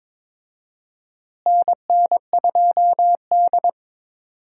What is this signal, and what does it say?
Morse code, heard as a keyed tone, about 22 wpm, 700 Hz. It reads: NN2D
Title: USA Call Signs - 4 Characters - Repeated 3x - 22wpm